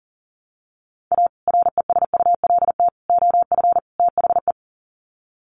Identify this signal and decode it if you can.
Morse code, heard as a keyed tone, about 40 wpm, 700 Hz. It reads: A RESULT OF THE
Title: Sets of 4 Words 40wpm